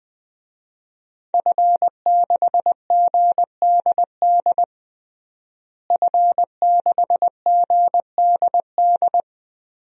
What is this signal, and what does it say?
Morse code, heard as a keyed tone, about 20 wpm, 700 Hz. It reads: F6GDD F6GDD
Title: DXCC-100 Call Signs - 5 Characters - Repeated 3x - 20wpm